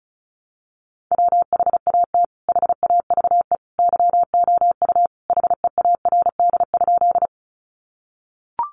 Morse code, heard as a keyed tone, about 34 wpm, 700 Hz. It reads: WHAT HAVE YOU HEARD?
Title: Sentences from Top 400 Words 35wpm